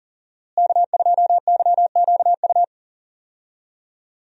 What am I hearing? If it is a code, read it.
Morse code, heard as a keyed tone, about 40 wpm, 700 Hz. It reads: K2YQU
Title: USA Call Signs - 5 Characters - Repeated 3x - 40wpm